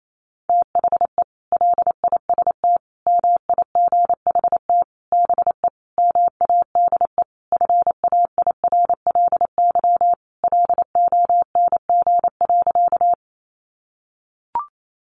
Morse code, heard as a keyed tone, about 28 wpm, 700 Hz. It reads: THE LIST MIGHT BE MADE FAIRLY LONG.
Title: Sentences from Top 1100 Words 28wpm